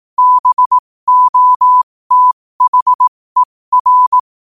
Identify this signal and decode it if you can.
Morse code, heard as a keyed tone, about 18 wpm, spaced as for 15 wpm, 1000 Hz. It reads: BOTHER
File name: qotdfiles / 15wpm